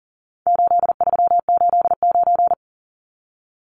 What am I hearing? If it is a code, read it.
Morse code, heard as a keyed tone, about 37 wpm, 700 Hz. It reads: 8389
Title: Numbers - 4-Digit 40wpm